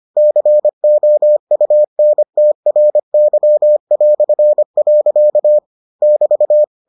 Morse code, heard as a keyed tone, about 25 wpm, 600 Hz. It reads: COUNTRY". <BT>